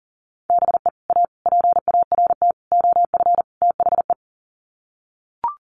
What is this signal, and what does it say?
Morse code, heard as a keyed tone, about 40 wpm, 700 Hz. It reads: BE A PART OF THE
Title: Sets of 5 Words Encore 40wpm